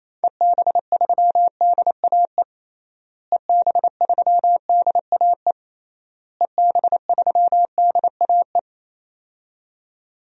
Morse code, heard as a keyed tone, about 28 wpm, 700 Hz. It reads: EB3DAE EB3DAE EB3DAE
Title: DXCC-100 Call Signs - 6 Characters - Repeated 3x - 28wpm